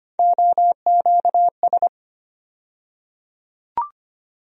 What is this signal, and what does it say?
Morse code, heard as a keyed tone, about 25 wpm, 700 Hz. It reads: OQS
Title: USA Call Sign Postfixes - 3 Letters 25wpm